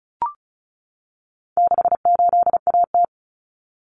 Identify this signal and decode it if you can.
Morse code, heard as a keyed tone, about 34 wpm, 700 Hz. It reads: 68AT
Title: Common Cut Numbers - 4-Digit 35wpm